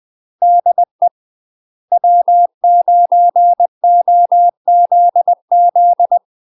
Morse code, heard as a keyed tone, about 20 wpm, 700 Hz. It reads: DE W9OZZ
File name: UK MASTER QSOs TRAINING / 20WPM / UK MASTER QSOs 0066